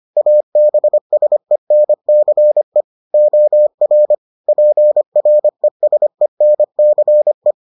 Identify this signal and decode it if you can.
Morse code, heard as a keyed tone, about 25 wpm, 600 Hz. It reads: ABSENCE OR PRESENCE